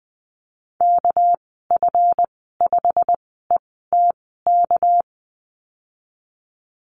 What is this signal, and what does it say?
Morse code, heard as a keyed tone, about 20 wpm, spaced as for 14 wpm, 700 Hz. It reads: KF5ETK
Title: 231 Lesson 37g - Letter x - Call Signs 20wpm